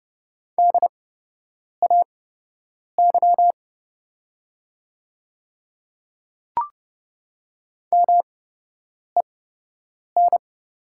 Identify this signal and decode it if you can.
Morse code, heard as a keyed tone, about 30 wpm, spaced as for 8 wpm, 700 Hz. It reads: DAY MEN
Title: 8x Character Spacing - Top 300 Words - 30wpm